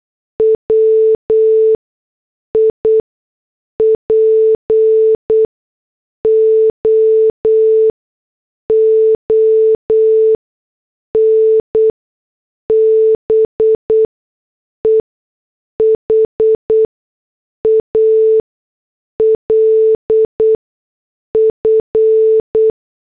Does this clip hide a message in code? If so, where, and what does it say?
Yes: Morse code from 0.4 to 22.7 s, heard as a keyed tone, about 8 wpm, spaced as for 6 wpm, 440 Hz. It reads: WIPO ON BEHALF